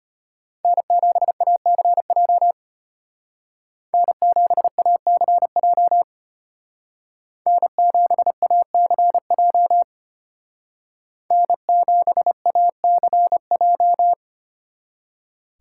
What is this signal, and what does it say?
Morse code, heard as a keyed tone, about 33 wpm, 700 Hz. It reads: N7ACJ N7ACJ N7ACJ N7ACJ